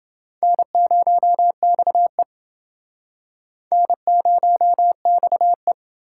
Morse code, heard as a keyed tone, about 27 wpm, 700 Hz. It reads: N0XE N0XE